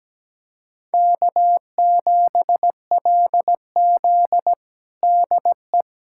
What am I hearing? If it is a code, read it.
Morse code, heard as a keyed tone, about 17 wpm, 700 Hz. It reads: K7LZ DE
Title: USA Callsign DE Callsign - 4 Characters - 17wpm